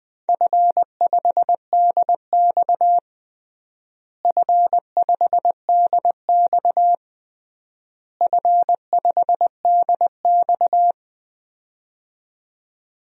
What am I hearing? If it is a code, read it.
Morse code, heard as a keyed tone, about 21 wpm, 700 Hz. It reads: F5DX F5DX F5DX
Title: DXCC-100 Call Signs - 4 Characters - Repeated 3x - 20wpm